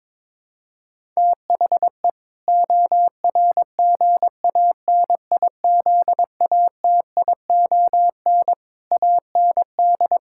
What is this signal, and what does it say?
Morse code, heard as a keyed tone, about 22 wpm, 700 Hz. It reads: THE ORGANIZATION AND
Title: Sentences from Top 1000 Words - Review Part 6 - 22wpm